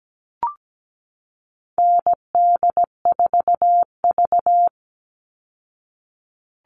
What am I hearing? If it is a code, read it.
Morse code, heard as a keyed tone, about 17 wpm, 700 Hz. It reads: ND4V
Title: Call Signs Repeated 2x - 17wpm